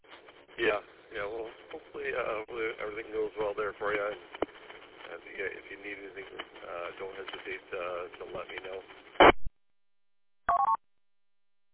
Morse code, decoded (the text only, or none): none